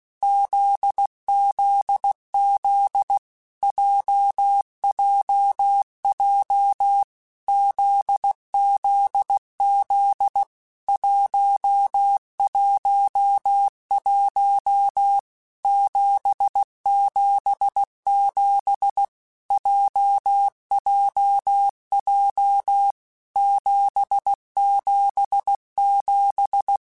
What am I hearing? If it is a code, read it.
Morse code, heard as a keyed tone, about 16 wpm, 800 Hz. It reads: ZZZ JJJ ZZZ 111 777 JJJ 777
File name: Basic Letter Groups By7Groups / JZ17 / JZ17@15